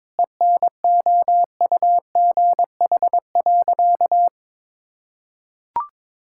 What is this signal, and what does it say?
Morse code, heard as a keyed tone, about 22 wpm, 700 Hz. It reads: ENOUGH.
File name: practice / 22wpm / Sentences with 4-Word Phrases - 22wpm